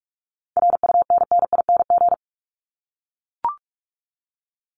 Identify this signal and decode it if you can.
Morse code, heard as a keyed tone, about 44 wpm, 700 Hz. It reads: RUNNING
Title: ING Suffix - Top 100 Words with Base Words 45wpm